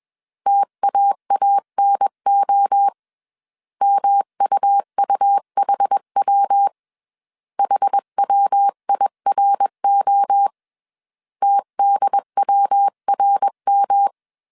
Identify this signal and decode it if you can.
Morse code, heard as a keyed tone, about 21 wpm, 800 Hz. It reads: TAANO MUUHW HWIRO TDWRM
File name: CW / DET / 1 3 1500C S B 20wpm 1754